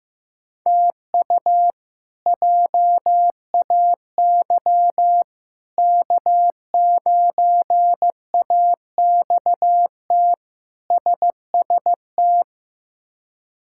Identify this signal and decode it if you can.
Morse code, heard as a keyed tone, about 15 wpm, 700 Hz. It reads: TU JAY K9AXT SST
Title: K1USN SST QSO Practice - 15wpm